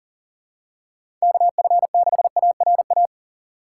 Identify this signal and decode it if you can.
Morse code, heard as a keyed tone, about 38 wpm, 700 Hz. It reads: KF6ARA
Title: USA Call Signs - 6 Characters - 4x Character Spacing - 40wpm